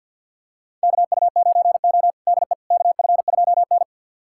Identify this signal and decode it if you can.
Morse code, heard as a keyed tone, about 52 wpm, 690 Hz. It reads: KU9O DE KF2N
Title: USA Callsign DE Callsign - 4 Characters - 50wpm